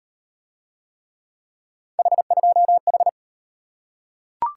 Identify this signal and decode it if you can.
Morse code, heard as a keyed tone, about 38 wpm, 700 Hz. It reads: H2H